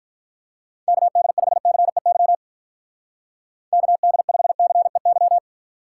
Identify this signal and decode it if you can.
Morse code, heard as a keyed tone, about 44 wpm, 700 Hz. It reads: KD5CEY KD5CEY
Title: Speed Racing - USA 2x3 Call Signs 35wpm